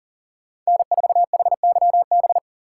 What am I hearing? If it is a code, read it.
Morse code, heard as a keyed tone, about 48 wpm, 700 Hz. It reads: N4HYB
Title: Speed Racing - USA 1x3 Call Signs 40wpm